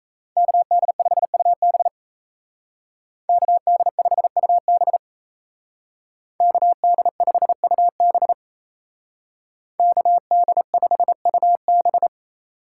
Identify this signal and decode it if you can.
Morse code, heard as a keyed tone, about 35 wpm, 700 Hz. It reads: KD5UB KD5UB KD5UB KD5UB
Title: Speed Racing - USA 2x2 Call Signs 28wpm